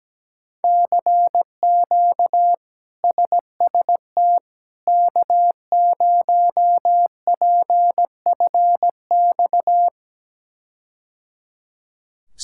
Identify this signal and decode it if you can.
Morse code, heard as a keyed tone, about 17 wpm, 700 Hz. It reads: CQ SST K0PFX
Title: K1USN SST QSO Practice - 17wpm